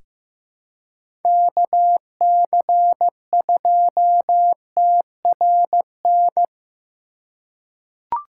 Morse code, heard as a keyed tone, about 15 wpm, 700 Hz. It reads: KC2TRN
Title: USA Call Signs - 6 Characters - 3x Character Spacing - 15wpm